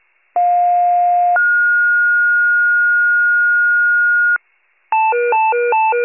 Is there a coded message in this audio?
Yes: two-tone page from 0.4 to 4.4 s, heard as a one-second tone, then a three-second tone, about 710 then 1430 Hz.